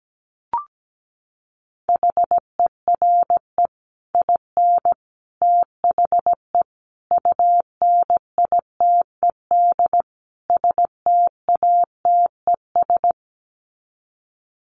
Morse code, heard as a keyed tone, about 17 wpm, 700 Hz. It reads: HERE IN THE UNITED STATES
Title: Sets of 5 Words 17wpm